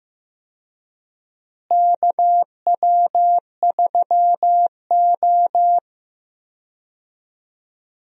Morse code, heard as a keyed tone, about 15 wpm, 700 Hz. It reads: KW3O
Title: USA Call Signs - 4 Characters - Repeated 3x - 15wpm